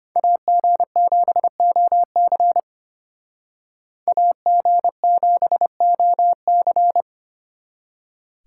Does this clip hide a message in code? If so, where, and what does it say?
Yes: Morse code from 0.2 to 7.0 s, heard as a keyed tone, about 33 wpm, 700 Hz. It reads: AG7OC AG7OC